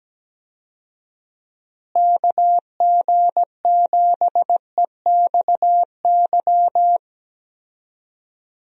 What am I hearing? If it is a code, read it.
Morse code, heard as a keyed tone, about 17 wpm, 700 Hz. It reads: KG7EXY
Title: USA Call Signs - 6 Characters - Repeated 2x - 17wpm